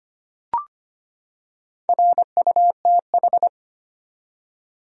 Morse code, heard as a keyed tone, about 25 wpm, 700 Hz. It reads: RUTH
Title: USA Common Names - 25wpm